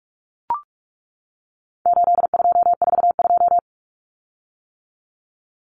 Morse code, heard as a keyed tone, about 44 wpm, 700 Hz. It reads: 8242
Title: Common Cut Numbers - 4-Digit 45wpm